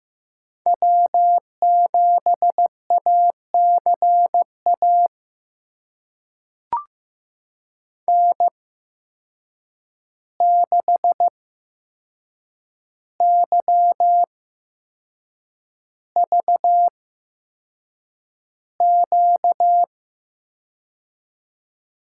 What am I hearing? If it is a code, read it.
Morse code, heard as a keyed tone, about 15 wpm, spaced as for 6 wpm, 700 Hz. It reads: W7ACA N6YVQ
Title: USA Call Signs - 5 Characters - 8x Character Spacing - 15wpm